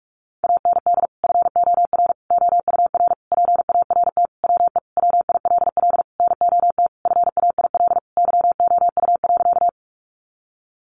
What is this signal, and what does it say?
Morse code, heard as a keyed tone, about 44 wpm, 700 Hz. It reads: AND FOR OUR PART WE WILL NOT FAIL YOU.
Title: Sentences from Top 1100 Words 45wpm